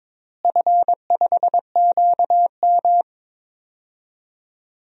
Morse code, heard as a keyed tone, about 21 wpm, 700 Hz. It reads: F5QM
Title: DXCC-100 Call Signs - 4 Characters - Repeated 2x - 22wpm